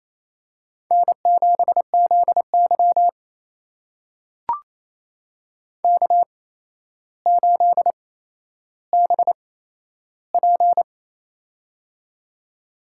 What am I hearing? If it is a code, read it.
Morse code, heard as a keyed tone, about 28 wpm, spaced as for 12 wpm, 700 Hz. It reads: N7ZY K8BP
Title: USA Call Signs - 8x Character Spacing - 28wpm